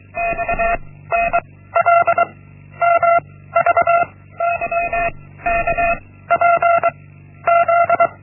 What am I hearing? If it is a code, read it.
Morse code, heard as a keyed tone, about 23 wpm, 680 Hz. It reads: X N L M V Y K P Z
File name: Linksmakalnis-1